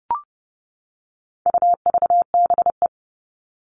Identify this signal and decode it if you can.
Morse code, heard as a keyed tone, about 30 wpm, 700 Hz. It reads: UVBE
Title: All Cut Numbers - 4-Digit 30wpm